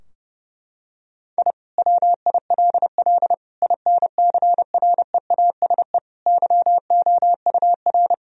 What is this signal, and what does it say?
Morse code, heard as a keyed tone, about 32 wpm, 700 Hz. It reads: I WILL INCREASE YOUR
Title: Sentences from Top 1000 Words - Review Part 4 - 30wpm